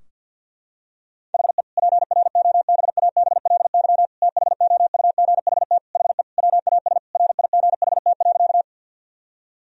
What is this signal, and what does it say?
Morse code, heard as a keyed tone, about 49 wpm, 700 Hz. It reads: HE PROBABLY THOUGHT HE WAS RIGHT.
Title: Sentences from Top 1000 Words - Review Part 4 - 50wpm